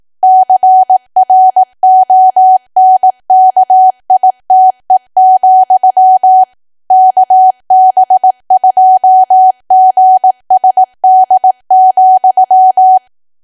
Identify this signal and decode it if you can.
Morse code, heard as a keyed tone, about 18 wpm, 750 Hz. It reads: CRONKITE, KB2GSD,
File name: Archive / 18 WPM / 18WPM